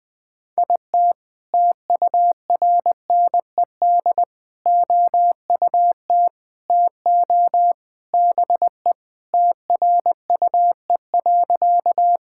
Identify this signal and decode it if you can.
Morse code, heard as a keyed tone, about 20 wpm, 700 Hz. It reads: IT TURNED OUT TO BE TRUE.
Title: Sentences with 3-Word Phrases - 20wpm